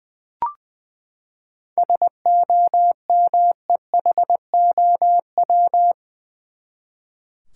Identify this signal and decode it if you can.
Morse code, heard as a keyed tone, about 20 wpm, 700 Hz. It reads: SOMEHOW
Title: Seven Letter Words - 20wpm